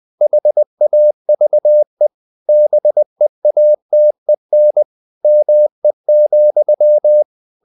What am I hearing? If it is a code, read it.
Morse code, heard as a keyed tone, about 20 wpm, 600 Hz. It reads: HAVE BEATEN ME,
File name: the old man and the sea part 2 20wpm